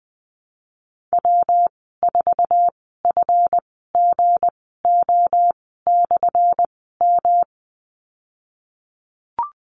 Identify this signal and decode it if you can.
Morse code, heard as a keyed tone, about 20 wpm, spaced as for 15 wpm, 700 Hz. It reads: W4FGO/M
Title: 179 Lesson 29g - Slash - Call Signs 20wpm